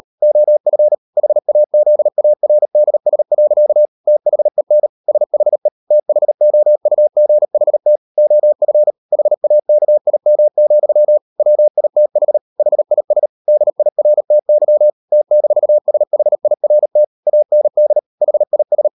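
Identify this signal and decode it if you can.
Morse code, heard as a keyed tone, about 37 wpm, 600 Hz. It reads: OF HAZARDS. THEN SHE THOUGHT OF HAKIM, WITH HIS DIRTY T-SHIRT AND HIS
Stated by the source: Morse code practice at 38 WPM, sending Ken Follett's novel Never in English, from CW med Gnister.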